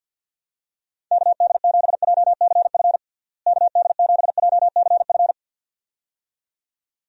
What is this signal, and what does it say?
Morse code, heard as a keyed tone, about 51 wpm, 700 Hz. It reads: KD7JCF KD7JCF
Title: USA Call Signs - 6 Characters - Repeated 2x - 50wpm